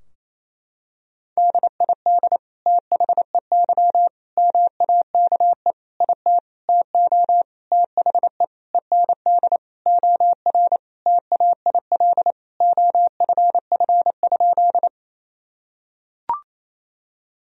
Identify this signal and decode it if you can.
Morse code, heard as a keyed tone, about 28 wpm, 700 Hz. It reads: DID THEY MAKE IT TO THE END OR TAIL OFF?
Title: Sentences from Top 1000 Words - Review Part 3 - 28wpm